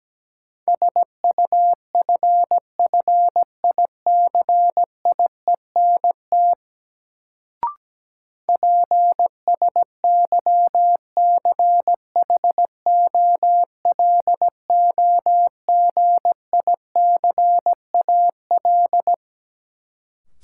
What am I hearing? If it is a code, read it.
Morse code, heard as a keyed tone, about 17 wpm, 700 Hz. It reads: SUFFICIENT PSYCHOLOGICAL
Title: Sentences from Top 1400 Words - New Word Drill - 17wpm